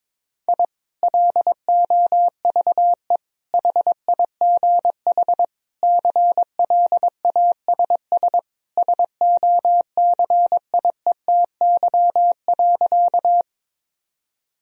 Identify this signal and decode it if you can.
Morse code, heard as a keyed tone, about 22 wpm, 700 Hz. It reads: I LOVE HIGH CLASS SOCIETY.
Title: Sentences from Top 1000 Words - Review Part 4 - 22wpm